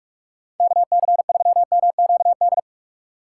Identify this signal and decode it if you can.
Morse code, heard as a keyed tone, about 46 wpm, 700 Hz. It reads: KC3MQD
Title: Speed Racing - USA Call Signs 45wpm